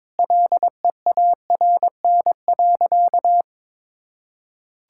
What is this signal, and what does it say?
Morse code, heard as a keyed tone, about 22 wpm, 700 Hz. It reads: LEARN.
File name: practice / 22wpm / Sentences with 3-Word Phrases - 22wpm